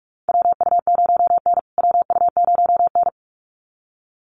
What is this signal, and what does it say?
Morse code, heard as a keyed tone, about 44 wpm, 700 Hz. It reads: WU0N WU0N
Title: USA Call Signs Repeated 2x 45wpm